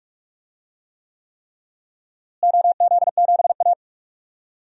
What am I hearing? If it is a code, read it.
Morse code, heard as a keyed tone, about 44 wpm, 700 Hz. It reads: OZ7A
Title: DXCC-100 Call Signs - 4 Characters - Repeated 3x - 45wpm